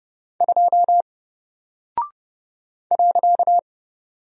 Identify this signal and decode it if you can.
Morse code, heard as a keyed tone, about 29 wpm, 700 Hz. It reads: 2 .
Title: Single Character (Letter:Number:Punctuation:Prosign) Alt 30wpm